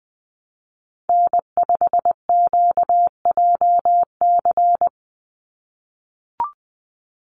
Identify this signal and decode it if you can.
Morse code, heard as a keyed tone, about 19 wpm, 700 Hz. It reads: N5QJC
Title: USA Call Signs 20wpm